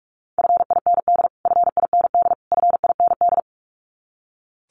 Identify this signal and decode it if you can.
Morse code, heard as a keyed tone, about 44 wpm, 700 Hz. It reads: FIND FIND FIND
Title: Top 500 Words Repeated 45wpm